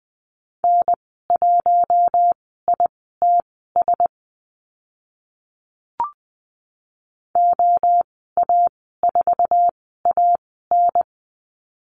Morse code, heard as a keyed tone, about 20 wpm, spaced as for 14 wpm, 700 Hz. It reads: N1ITS OA4AN
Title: 048 Lesson 9g - Number 4 - Call Signs 20wpm